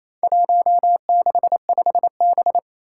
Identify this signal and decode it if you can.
Morse code, heard as a keyed tone, about 27 wpm, 700 Hz. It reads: 165B
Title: Four Challenging Letter-Number Combinations 28wpm